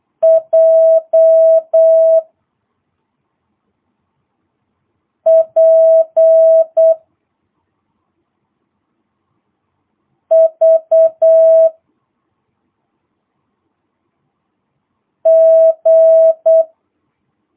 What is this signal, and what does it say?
Morse code, heard as a keyed tone, about 8 wpm, 650 Hz. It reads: J P V G